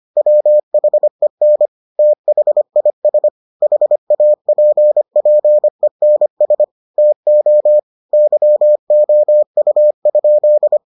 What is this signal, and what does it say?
Morse code, heard as a keyed tone, about 25 wpm, 600 Hz. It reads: WHEN THIS HAPPENS TO YOU?